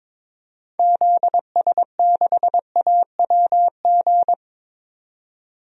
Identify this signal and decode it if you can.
Morse code, heard as a keyed tone, about 22 wpm, 700 Hz. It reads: ZS6AWG
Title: DXCC-100 Call Signs - 6 Characters - 22wpm